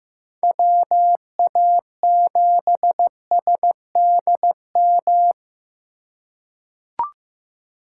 Morse code, heard as a keyed tone, about 16 wpm, 700 Hz. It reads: WA7SDM
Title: USA Call Signs - 4x Character Spacing - 15wpm